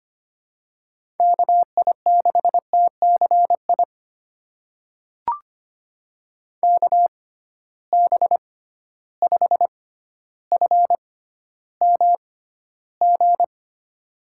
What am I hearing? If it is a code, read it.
Morse code, heard as a keyed tone, about 25 wpm, spaced as for 13 wpm, 700 Hz. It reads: KI6TCI KB5FMG
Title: USA Call Signs - 6x Character Spacing - 25wpm